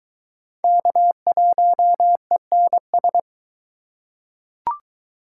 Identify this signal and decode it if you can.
Morse code, heard as a keyed tone, about 21 wpm, 700 Hz. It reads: K1ENS